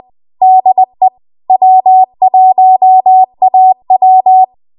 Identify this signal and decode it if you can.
Morse code, heard as a keyed tone, about 20 wpm, 750 Hz. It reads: DE W1AW
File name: Archive / 20 WPM / 20WPM